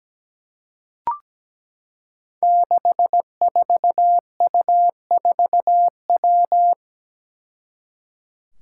Morse code, heard as a keyed tone, about 17 wpm, 700 Hz. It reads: 64U4W
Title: Five Challenging Letter-Number Combinations 17wpm